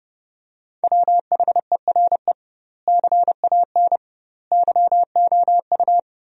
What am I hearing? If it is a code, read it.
Morse code, heard as a keyed tone, about 30 wpm, 700 Hz. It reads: WHERE CAN YOU
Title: 2x Word-Spacing - Sentences from Top 500 Words - 30wpm